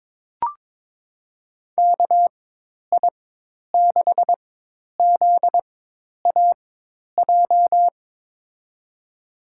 Morse code, heard as a keyed tone, about 22 wpm, spaced as for 10 wpm, 700 Hz. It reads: KI6ZAJ
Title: USA Call Signs - 6 Characters - 4x Character Spacing - 22wpm